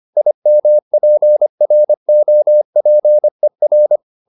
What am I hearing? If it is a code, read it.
Morse code, heard as a keyed tone, about 25 wpm, 600 Hz. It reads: IMPROPER